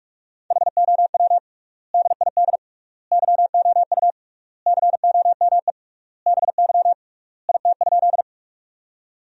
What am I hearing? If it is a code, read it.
Morse code, heard as a keyed tone, about 46 wpm, 700 Hz. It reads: HOW DID YOU COME BY IT?
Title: 3x Word-Spacing - Sentences from Top 100 Words - 45wpm